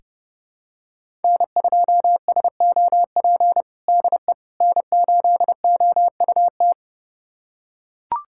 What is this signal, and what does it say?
Morse code, heard as a keyed tone, about 30 wpm, 700 Hz. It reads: N2SOP DE N8OUT
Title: USA Callsign DE Callsign - 5 Characters - 30wpm